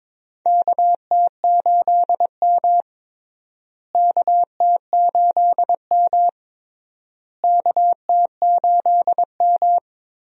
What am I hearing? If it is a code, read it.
Morse code, heard as a keyed tone, about 22 wpm, 700 Hz. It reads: KT8M KT8M KT8M
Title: USA Call Signs - 4 Characters - Repeated 3x - 22wpm